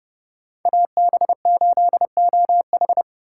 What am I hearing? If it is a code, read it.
Morse code, heard as a keyed tone, about 29 wpm, 700 Hz. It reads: AB8OH